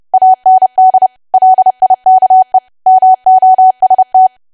Morse code, heard as a keyed tone, about 30 wpm, 750 Hz. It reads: AND LIKE MOST
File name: cwa / W1AW-Level3 / 30WPM